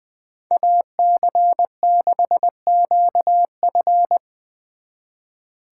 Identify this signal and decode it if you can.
Morse code, heard as a keyed tone, about 24 wpm, 700 Hz. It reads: AC6QF
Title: Speed Racing - USA 2x2 Call Signs 20wpm